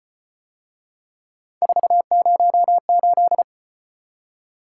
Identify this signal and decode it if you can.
Morse code, heard as a keyed tone, about 33 wpm, 700 Hz. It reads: 408